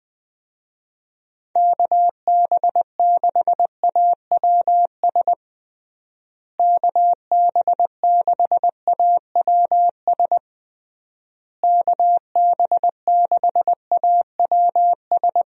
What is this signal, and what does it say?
Morse code, heard as a keyed tone, about 20 wpm, 700 Hz. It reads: KB6AWS KB6AWS KB6AWS
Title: USA Call Signs - 6 Characters - Repeated 3x - 20wpm